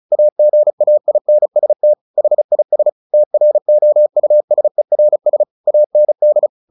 Morse code, heard as a keyed tone, about 35 wpm, 600 Hz. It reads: AGAINST HIS TROUSERS AND
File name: the old man and the sea / the old man and the sea 1 35wpm